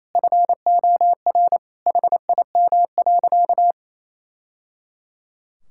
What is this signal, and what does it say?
Morse code, heard as a keyed tone, about 28 wpm, 700 Hz. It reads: FOR HIM.
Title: Sentences from Top 1000 Words - Review Part 6 - 28wpm